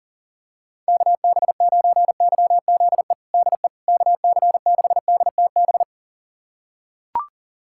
Morse code, heard as a keyed tone, about 37 wpm, 700 Hz. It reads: KB9YZE DE KC6DTB
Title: USA Callsign DE Callsign - 6 Characters - 40wpm